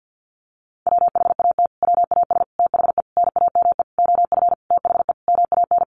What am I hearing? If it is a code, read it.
Morse code, heard as a keyed tone, about 50 wpm, 700 Hz. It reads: WHAT WAS THE NAME OF THE MAN
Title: Sentences from Top 300 Words 50wpm